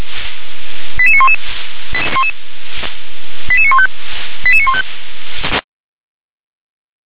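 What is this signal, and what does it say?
Five-tone selective call (ZVEI), heard as quick runs of five tones, tones of about 70 ms. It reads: 88911 88915 88915
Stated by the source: ZVEI five-tone call received on an SDR.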